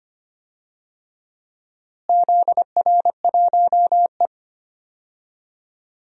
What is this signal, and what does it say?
Morse code, heard as a keyed tone, about 25 wpm, 700 Hz. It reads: ZR1E
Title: DXCC-100 Call Signs - 4 Characters - Repeated 3x - 25wpm